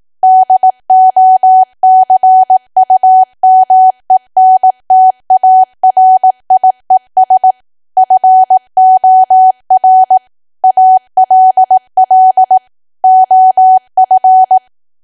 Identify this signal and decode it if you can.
Morse code, heard as a keyed tone, about 18 wpm, 750 Hz. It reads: DOCUMENTARIES FOR ALL OF
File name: Archive / 18 WPM / 18WPM